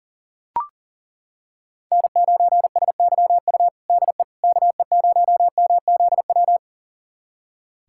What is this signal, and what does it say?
Morse code, heard as a keyed tone, about 40 wpm, 700 Hz. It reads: N9SYU DE KE0MZW
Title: USA Callsign DE Callsign 40wpm